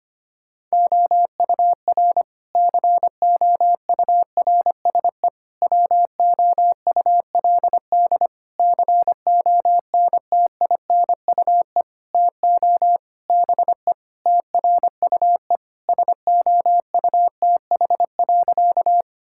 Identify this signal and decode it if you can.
Morse code, heard as a keyed tone, about 25 wpm, 700 Hz. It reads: OUR COURSE WOULD CONTINUE TO BE TRUE SOUTH.
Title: Sentences from Top 1000 Words - Review Part 6 - 25wpm